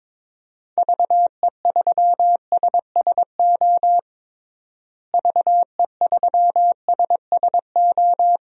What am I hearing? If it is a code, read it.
Morse code, heard as a keyed tone, about 22 wpm, 700 Hz. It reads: VE3SSO VE3SSO
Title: DXCC-100 Call Signs - 6 Characters - Repeated 3x - 22wpm